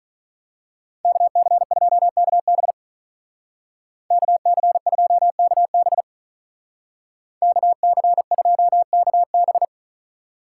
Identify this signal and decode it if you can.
Morse code, heard as a keyed tone, about 43 wpm, 700 Hz. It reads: KC2KB KC2KB KC2KB
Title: Speed Racing - USA Call Signs 35wpm